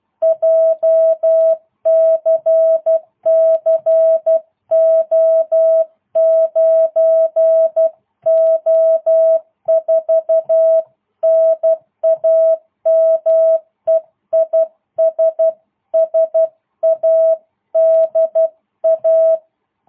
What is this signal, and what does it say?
Morse code, heard as a keyed tone, about 12 wpm, 650 Hz. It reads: JCC O9O4 NAME IS SADA